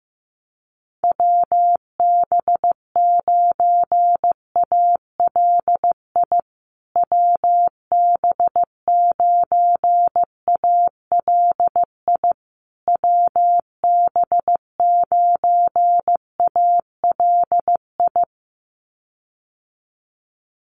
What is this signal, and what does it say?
Morse code, heard as a keyed tone, about 15 wpm, 700 Hz. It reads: WB9ALI WB9ALI WB9ALI
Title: USA Call Signs Repeated 3x 15wpm